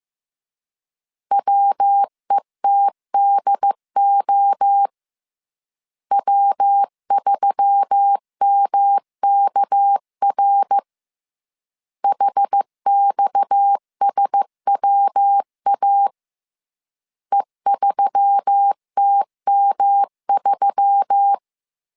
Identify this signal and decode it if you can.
Morse code, heard as a keyed tone, about 15 wpm, 800 Hz. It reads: WETDO W3MKR HXSWA E3TM3
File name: CW / DET / 1 6 1500C S B 14wpm 2958